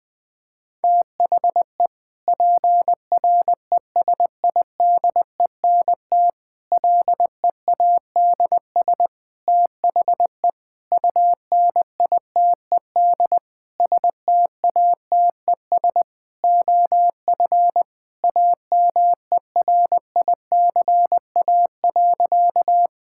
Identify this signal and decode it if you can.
Morse code, heard as a keyed tone, about 21 wpm, 700 Hz. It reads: THE PRESIDENT LEADS THE UNITED STATES OF AMERICA.
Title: Sentences with 5-Word Phrases - Spoken Beginning - 20wpm